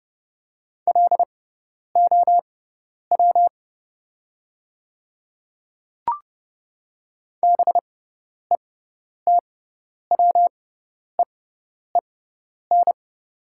Morse code, heard as a keyed tone, about 31 wpm, spaced as for 10 wpm, 700 Hz. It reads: LOW BETWEEN
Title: 6x Character Spacing - Top 300 Words - 30wpm